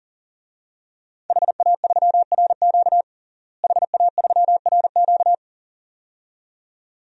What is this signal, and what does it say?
Morse code, heard as a keyed tone, about 37 wpm, 700 Hz. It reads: HA3RQ HA3RQ
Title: DXCC-100 Call Signs - 5 Characters - Repeated 2x - 40wpm